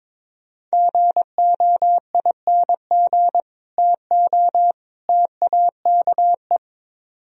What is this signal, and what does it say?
Morse code, heard as a keyed tone, about 24 wpm, 700 Hz. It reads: GOING TO TAKE
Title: Speed-Racing - Sets of 3 Words 22wpm